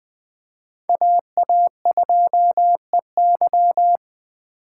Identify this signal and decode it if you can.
Morse code, heard as a keyed tone, about 21 wpm, 700 Hz. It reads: AA2EY